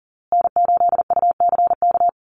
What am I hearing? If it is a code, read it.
Morse code, heard as a keyed tone, about 40 wpm, 700 Hz. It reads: N8UCK
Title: USA Call Signs Repeated 2x 40wpm